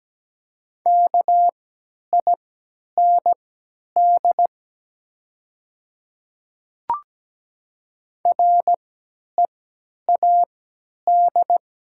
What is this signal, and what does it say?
Morse code, heard as a keyed tone, about 17 wpm, spaced as for 10 wpm, 700 Hz. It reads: KIND READ
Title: 3x Character Spacing - Top 200 Words - 17wpm